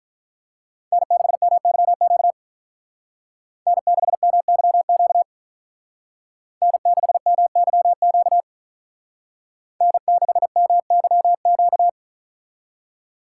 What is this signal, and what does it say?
Morse code, heard as a keyed tone, about 43 wpm, 700 Hz. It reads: N6MYQ N6MYQ N6MYQ N6MYQ
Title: Speed Racing - USA Call Signs 35wpm